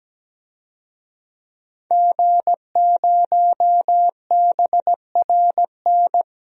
Morse code, heard as a keyed tone, about 17 wpm, 700 Hz. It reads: G0BRN
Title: DXCC-100 Call Signs - 5 Characters - 17wpm